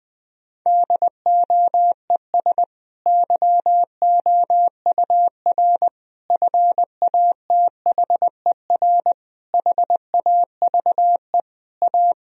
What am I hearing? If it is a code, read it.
Morse code, heard as a keyed tone, about 20 wpm, 700 Hz. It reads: DOES YOUR FATHER HAVE A
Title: Sentences from Top 1000 Words - Review Part 3 - 20wpm